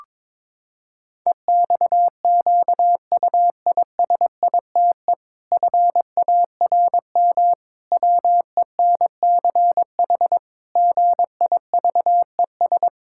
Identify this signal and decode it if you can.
Morse code, heard as a keyed tone, about 22 wpm, 700 Hz. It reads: EXQUISITE FARM WENCH GIVES
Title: Pangram Challenge v2 - 22wpm